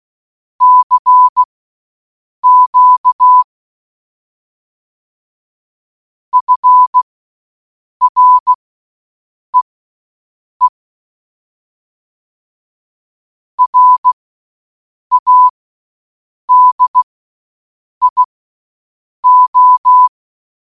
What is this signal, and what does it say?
Morse code, heard as a keyed tone, about 16 wpm, spaced as for 7 wpm, 1000 Hz. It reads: CQ FREE RADIO